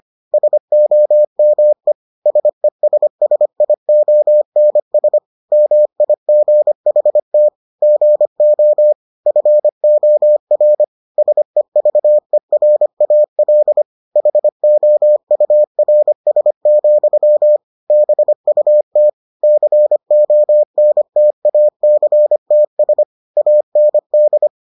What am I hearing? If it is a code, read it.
Morse code, heard as a keyed tone, about 25 wpm, 600 Hz. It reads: SOME SESSIONS MIGHT GO FOR SEVERAL HOURS, BUT CONTACTS AND